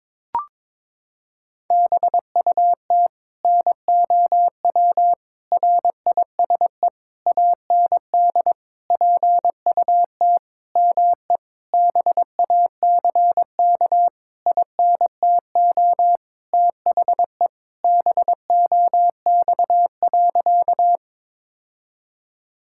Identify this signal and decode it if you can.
Morse code, heard as a keyed tone, about 22 wpm, 700 Hz. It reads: BUT NOW RISE AND PUT ME BACK INTO THE BOX.
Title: Sentences from Top 1000 Words - Review Part 6 - 22wpm